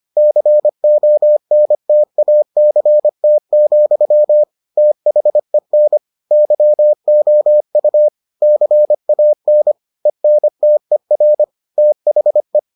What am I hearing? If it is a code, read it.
Morse code, heard as a keyed tone, about 25 wpm, 600 Hz. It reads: CONTACT, THEN YOU CAN ENTER THE